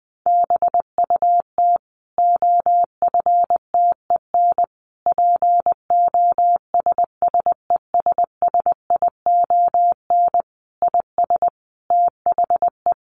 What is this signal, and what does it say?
Morse code, heard as a keyed tone, about 20 wpm, 700 Hz. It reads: BUT OFTEN POSSESSION IS THE
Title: Sentences from Top 1400 Words 20wpm